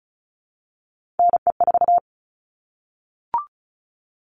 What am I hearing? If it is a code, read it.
Morse code, heard as a keyed tone, about 34 wpm, 700 Hz. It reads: NE4
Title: All Cut Numbers - 3-Digit 35wpm